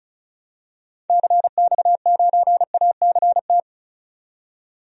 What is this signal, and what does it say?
Morse code, heard as a keyed tone, about 34 wpm, 700 Hz. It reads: CX9ACT
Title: DXCC-100 Call Signs - 6 Characters - 35wpm